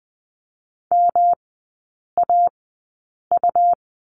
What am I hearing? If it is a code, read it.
Morse code, heard as a keyed tone, about 20 wpm, spaced as for 8 wpm, 700 Hz. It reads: MAU
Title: 127 Lesson 21e - Number 6 - Three Characters 20wpm